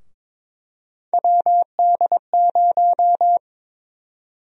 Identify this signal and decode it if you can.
Morse code, heard as a keyed tone, about 22 wpm, 700 Hz. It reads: WD0
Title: USA Call Sign Prefixes - 22wpm